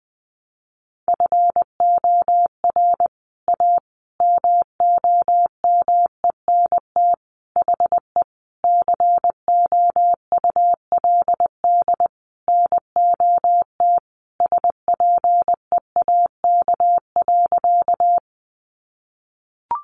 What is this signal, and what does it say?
Morse code, heard as a keyed tone, about 20 wpm, 700 Hz. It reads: FOR A MOMENT HE COULD NOT SPEAK.
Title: Sentences from Top 1100 Words 20wpm